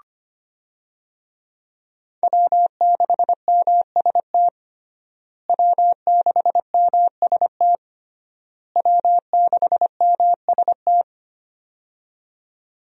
Morse code, heard as a keyed tone, about 25 wpm, 700 Hz. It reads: W6MST W6MST W6MST
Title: USA Call Signs - 5 Characters - Repeated 3x - 25wpm